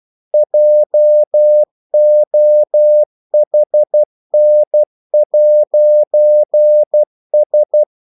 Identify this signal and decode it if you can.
Morse code, heard as a keyed tone, about 12 wpm, 600 Hz. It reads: JOHN'S
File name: CW / Practice / NationalCapitals12